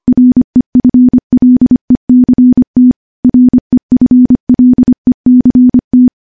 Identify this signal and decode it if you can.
Morse code, heard as a keyed tone, about 25 wpm, 250 Hz. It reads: REFLECT REFLECT
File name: morse-training-25WPM